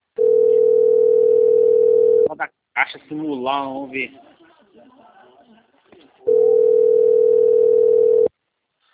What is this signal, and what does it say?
Ringback tone (North America), 440 + 480 Hz, with speech.
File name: rg-1600-unknown- - - .12153